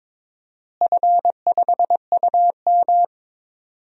Morse code, heard as a keyed tone, about 22 wpm, 700 Hz. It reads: F5UM
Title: DXCC-100 Call Signs - 4 Characters - Repeated 3x - 22wpm